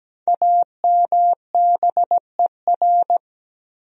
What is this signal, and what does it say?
Morse code, heard as a keyed tone, about 17 wpm, 700 Hz. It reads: AMBER
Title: USA Common Names - Max 5 Letters - 17wpm